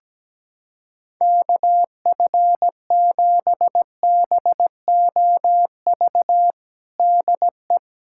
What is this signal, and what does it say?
Morse code, heard as a keyed tone, about 17 wpm, 700 Hz. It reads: KF7BOV DE
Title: USA Callsign DE Callsign 17wpm